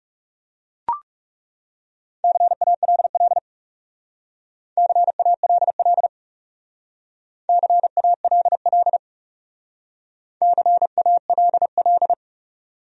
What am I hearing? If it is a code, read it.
Morse code, heard as a keyed tone, about 38 wpm, 700 Hz. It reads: CALL CALL CALL CALL